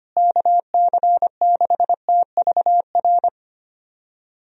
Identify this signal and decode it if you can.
Morse code, heard as a keyed tone, about 25 wpm, 700 Hz. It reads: KC6TVR
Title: USA Call Signs - 6 Characters - Repeated 3x - 25wpm